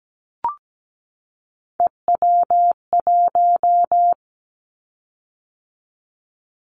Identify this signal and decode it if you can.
Morse code, heard as a keyed tone, about 17 wpm, 700 Hz. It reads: EW1
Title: Callsign Prefixes for Top 100 DXCC - 17wpm